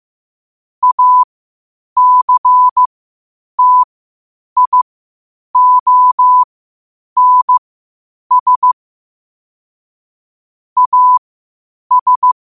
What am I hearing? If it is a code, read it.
Morse code, heard as a keyed tone, about 15 wpm, spaced as for 8 wpm, 1000 Hz. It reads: ACTIONS AS
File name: 8wpm